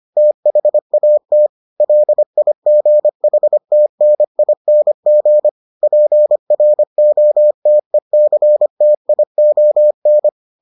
Morse code, heard as a keyed tone, about 25 wpm, 600 Hz. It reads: THAT LIGHTNING PROTECTION